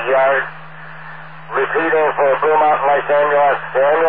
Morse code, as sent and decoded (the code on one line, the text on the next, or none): none